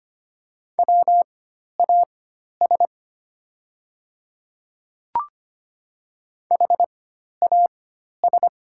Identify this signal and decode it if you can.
Morse code, heard as a keyed tone, about 25 wpm, spaced as for 12 wpm, 700 Hz. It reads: WAS HAS
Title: 4x Character Spacing - Top 100 Words - 25wpm